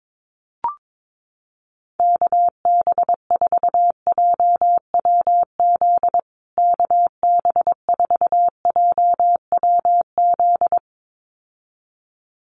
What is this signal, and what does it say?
Morse code, heard as a keyed tone, about 22 wpm, 700 Hz. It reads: KB4JWZ KB4JWZ
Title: USA Call Signs Repeated 2x 22wpm